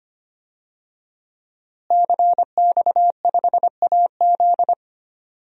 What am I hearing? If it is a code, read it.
Morse code, heard as a keyed tone, about 25 wpm, 700 Hz. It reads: CX5AZ
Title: DXCC-100 Call Signs - 5 Characters - Repeated 3x - 25wpm